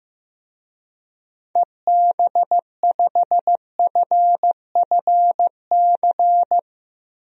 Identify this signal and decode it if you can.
Morse code, heard as a keyed tone, about 15 wpm, 700 Hz. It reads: EB5FFC
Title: DXCC-100 Call Signs - 6 Characters - Repeated 2x - 15wpm